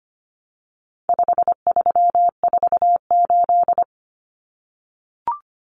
Morse code, heard as a keyed tone, about 25 wpm, 700 Hz. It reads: 5348
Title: Common Cut Numbers - 4-Digit 25wpm